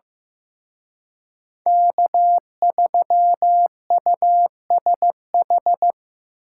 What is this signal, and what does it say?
Morse code, heard as a keyed tone, about 16 wpm, 700 Hz. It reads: K3USH
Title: Speed Racing - USA 1x3 Call Signs 15wpm